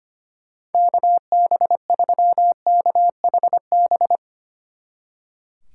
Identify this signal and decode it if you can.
Morse code, heard as a keyed tone, about 31 wpm, 700 Hz. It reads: KB3KHB
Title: Speed Racing - USA Call Signs 25wpm